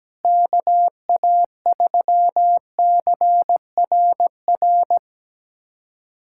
Morse code, heard as a keyed tone, about 17 wpm, 700 Hz. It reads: KA3CRR
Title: USA Callsign DE Callsign - 6 Characters - 17wpm